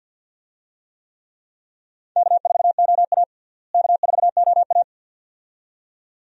Morse code, heard as a keyed tone, about 49 wpm, 700 Hz. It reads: K4OA K4OA
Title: USA Call Signs - 4 Characters - Repeated 2x - 50wpm